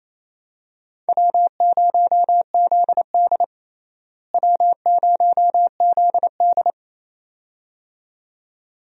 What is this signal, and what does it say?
Morse code, heard as a keyed tone, about 28 wpm, 700 Hz. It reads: W0ZD W0ZD
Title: USA Call Signs - 4 Characters - Repeated 2x - 28wpm